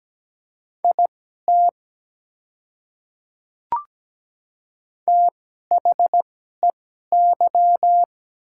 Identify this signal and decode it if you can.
Morse code, heard as a keyed tone, about 17 wpm, spaced as for 12 wpm, 700 Hz. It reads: IT THEY
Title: 2x Character Spacing - Top 100 Words - 17wpm